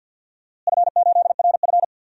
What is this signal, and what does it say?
Morse code, heard as a keyed tone, about 51 wpm, 700 Hz. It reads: F8RF
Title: DXCC-100 Call Signs - 4 Characters - Repeated 3x - 50wpm